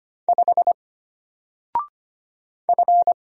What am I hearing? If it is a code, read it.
Morse code, heard as a keyed tone, about 25 wpm, 700 Hz. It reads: 5 F
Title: Single Letter-Number 25wpm